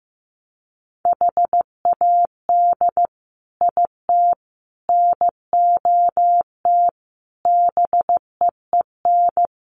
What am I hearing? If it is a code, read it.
Morse code, heard as a keyed tone, about 15 wpm, 700 Hz. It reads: HAD IT NOT BEEN
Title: Sets of 5 Words Encore2 15wpm